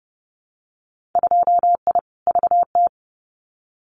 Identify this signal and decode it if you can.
Morse code, heard as a keyed tone, about 30 wpm, 700 Hz. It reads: 2I VT
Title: Winter Field Day Exchanges 30wpm